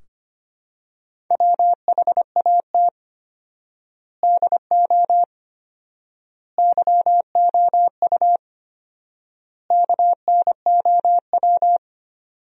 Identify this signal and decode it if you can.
Morse code, heard as a keyed tone, about 25 wpm, 700 Hz. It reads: WHAT DO YOU KNOW
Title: 4x Word-Spacing - Sentences from Top 100 Words - 25wpm